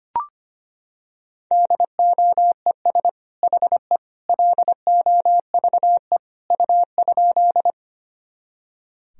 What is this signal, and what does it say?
Morse code, heard as a keyed tone, about 25 wpm, 700 Hz. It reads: DOES HE LOVE U?
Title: Sentences from Top 400 Words and Abbreviations 25wpm